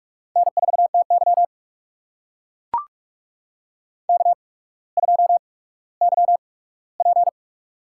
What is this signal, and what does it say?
Morse code, heard as a keyed tone, about 44 wpm, spaced as for 19 wpm, 700 Hz. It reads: N4TY K2YP
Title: USA Call Signs - 4 Characters - 8x Character Spacing - 45wpm